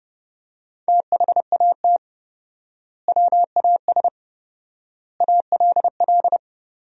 Morse code, heard as a keyed tone, about 30 wpm, 700 Hz. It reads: THAT WAS ALL
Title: 4x Word-Spacing - Sentences from Top 100 Words - 30wpm